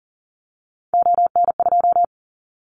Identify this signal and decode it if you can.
Morse code, heard as a keyed tone, about 40 wpm, 700 Hz. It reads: ON2
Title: Callsign Prefixes for Top 50 DXCC - 40wpm